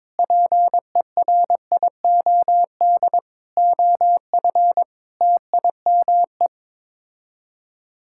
Morse code, heard as a keyed tone, about 22 wpm, 700 Hz. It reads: PERIOD OF TIME
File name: practice / 22wpm / Sets of 6 Words - 22wpm